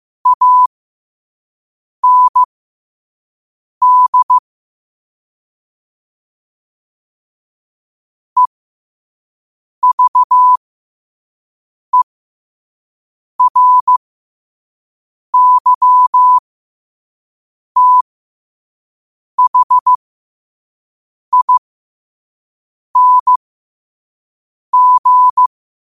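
Morse code, heard as a keyed tone, about 15 wpm, spaced as for 5 wpm, 1000 Hz. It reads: AND EVERYTHING
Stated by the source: Morse code quote of the day sent at 5 wpm.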